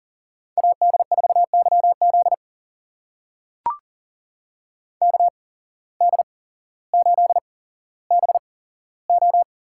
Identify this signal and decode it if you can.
Morse code, heard as a keyed tone, about 39 wpm, spaced as for 17 wpm, 700 Hz. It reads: AD4YZ KD8BO
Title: USA Call Signs - 5 Characters - 8x Character Spacing - 40wpm